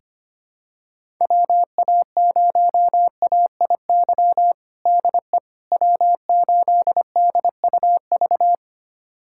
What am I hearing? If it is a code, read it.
Morse code, heard as a keyed tone, about 25 wpm, 700 Hz. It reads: WA0AIY DE W8DUV
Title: USA Callsign DE Callsign 25wpm